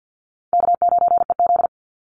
Morse code, heard as a keyed tone, about 49 wpm, 700 Hz. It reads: K9EZ
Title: NA SOTA Chaser Callsigns 50wpm